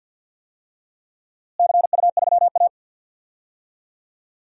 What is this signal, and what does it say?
Morse code, heard as a keyed tone, about 51 wpm, 700 Hz. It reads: CU3A
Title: DXCC-100 Call Signs - 4 Characters - Repeated 2x - 50wpm